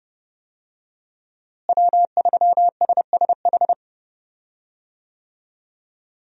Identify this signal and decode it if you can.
Morse code, heard as a keyed tone, about 30 wpm, 700 Hz. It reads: W3SSH
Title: USA Call Signs - 5 Characters - Repeated 2x - 30wpm